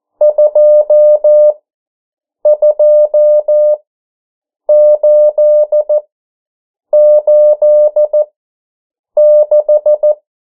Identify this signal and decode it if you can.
Morse code, heard as a keyed tone, about 14 wpm, spaced as for 7 wpm, 600 Hz. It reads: 22886
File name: morse mp3 / 09 urok 8